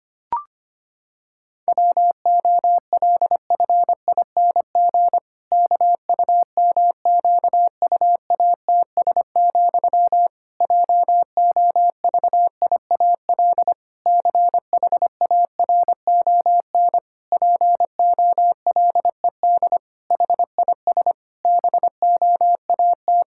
Morse code, heard as a keyed tone, about 25 wpm, 700 Hz. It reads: WOLFING KUMQUATS, JOVIAL CHARON POLED HIS BOAT
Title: Pangram Challenge v2 - 25wpm